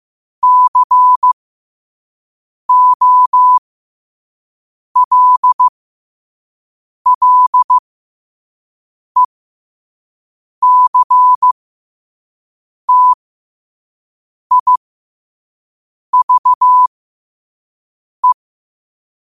Morse code, heard as a keyed tone, about 15 wpm, spaced as for 5 wpm, 1000 Hz. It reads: COLLECTIVE
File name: qotdfiles / 5wpm